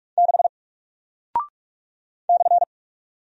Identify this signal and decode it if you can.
Morse code, heard as a keyed tone, about 46 wpm, 700 Hz. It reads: 6 /
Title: Single Character (Letter:Number:Punctuation:Prosign) Alt 45wpm